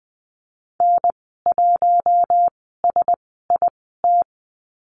Morse code, heard as a keyed tone, about 20 wpm, spaced as for 14 wpm, 700 Hz. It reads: N1SIT
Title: 042 Lesson 8g - Number 1 - Call Signs 20wpm